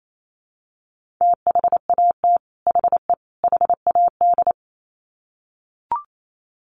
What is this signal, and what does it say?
Morse code, heard as a keyed tone, about 28 wpm, 700 Hz. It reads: THAT HE HAD
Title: Sets of 3 Words 28wpm